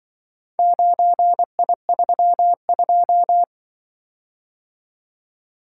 Morse code, heard as a keyed tone, about 24 wpm, 700 Hz. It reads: 9I32
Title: Speed-Racing - Four Letter-Number Combinations - 20wpm